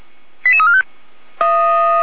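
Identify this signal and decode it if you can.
Five-tone selective call (ZVEI), heard as quick runs of five tones, tones of about 70 ms. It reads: 88345
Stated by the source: ZVEI five-tone selective call for 88288, received by SDR.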